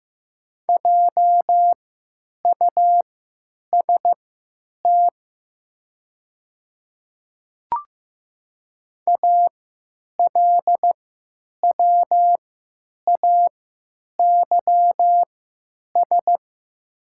Morse code, heard as a keyed tone, about 15 wpm, spaced as for 9 wpm, 700 Hz. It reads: JUST ALWAYS
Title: 3x Character Spacing - Top 300 Words - 15wpm